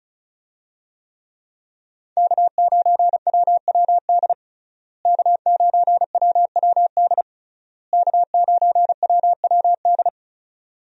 Morse code, heard as a keyed tone, about 35 wpm, 700 Hz. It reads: K9WWD K9WWD K9WWD
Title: USA Call Signs - 5 Characters - Repeated 3x - 35wpm